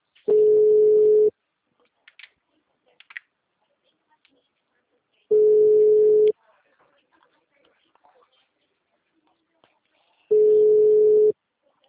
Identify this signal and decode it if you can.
Ringback tone (Europe, 425 Hz), with speech.